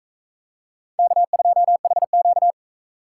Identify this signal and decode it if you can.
Morse code, heard as a keyed tone, about 44 wpm, 700 Hz. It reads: K2HQ